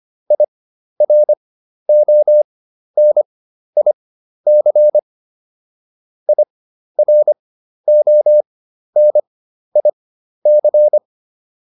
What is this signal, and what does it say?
Morse code, heard as a keyed tone, about 25 wpm, spaced as for 12 wpm, 600 Hz. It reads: IRONIC IRONIC